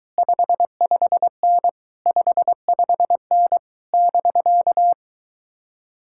Morse code, heard as a keyed tone, about 23 wpm, 700 Hz. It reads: 55N 55N <BK>